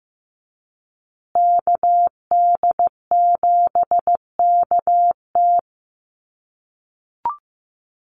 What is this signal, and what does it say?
Morse code, heard as a keyed tone, about 15 wpm, 700 Hz. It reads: KD7KT
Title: USA Call Signs Repeated 2x 15wpm